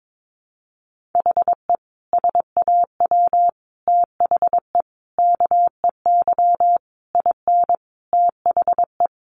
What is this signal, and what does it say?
Morse code, heard as a keyed tone, about 22 wpm, 700 Hz. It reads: HE SAW THE KEY IN THE
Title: Sentences from Top 1300 Words 22wpm